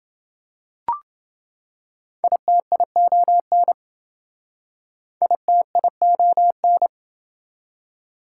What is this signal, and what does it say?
Morse code, heard as a keyed tone, about 27 wpm, 700 Hz. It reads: ITION ITION